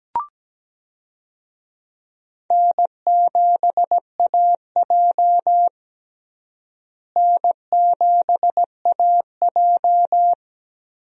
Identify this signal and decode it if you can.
Morse code, heard as a keyed tone, about 17 wpm, 700 Hz. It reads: N7AJ N7AJ
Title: USA Call Signs - 4 Characters - Repeated 2x - 17wpm